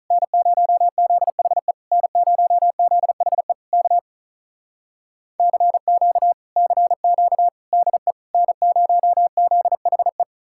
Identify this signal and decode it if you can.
Morse code, heard as a keyed tone, about 46 wpm, 700 Hz. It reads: N0ZHE N0ZHE K CQ CQ DE N0ZHE